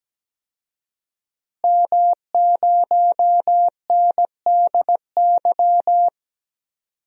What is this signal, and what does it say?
Morse code, heard as a keyed tone, about 17 wpm, 700 Hz. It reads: M0NDY
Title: DXCC-100 Call Signs - 5 Characters - Repeated 3x - 17wpm